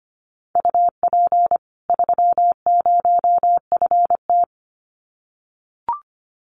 Morse code, heard as a keyed tone, about 25 wpm, 700 Hz. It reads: UP 30FT
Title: Common QSO Phrases 25wpm